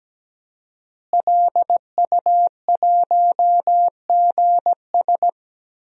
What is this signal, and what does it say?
Morse code, heard as a keyed tone, about 17 wpm, 700 Hz. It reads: LU1GS
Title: DXCC-100 Call Signs - 5 Characters - Repeated 3x - 17wpm